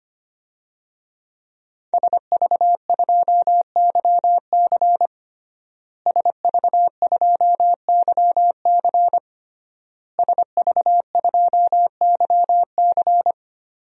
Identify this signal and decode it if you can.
Morse code, heard as a keyed tone, about 25 wpm, 700 Hz. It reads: SV2YC SV2YC SV2YC
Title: DXCC-100 Call Signs - 5 Characters - Repeated 3x - 25wpm